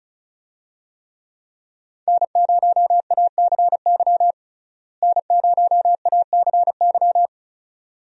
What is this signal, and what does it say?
Morse code, heard as a keyed tone, about 34 wpm, 700 Hz. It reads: N0ACY N0ACY
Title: USA Call Signs - 5 Characters - Repeated 2x - 35wpm